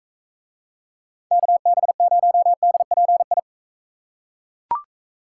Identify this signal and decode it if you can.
Morse code, heard as a keyed tone, about 39 wpm, 700 Hz. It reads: KB0DPI